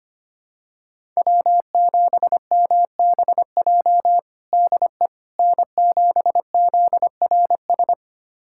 Morse code, heard as a keyed tone, about 25 wpm, 700 Hz. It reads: W7MBJ DE N7ZRS
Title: USA Callsign DE Callsign - 5 Characters - 25wpm